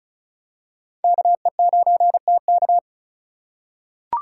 Morse code, heard as a keyed tone, about 36 wpm, 700 Hz. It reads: KE9TK
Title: USA Call Signs - 5 Characters - 2x Character Spacing - 35wpm